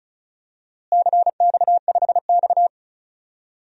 Morse code, heard as a keyed tone, about 34 wpm, 700 Hz. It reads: CX5X
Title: DXCC-100 Call Signs - 4 Characters - Repeated 2x - 35wpm